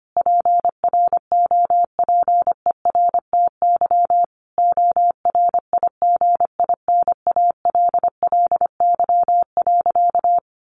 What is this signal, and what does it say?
Morse code, heard as a keyed tone, about 25 wpm, 700 Hz. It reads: PROPERTY ORIGINALLY.
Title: Sentences from Top 1500 Words 25wpm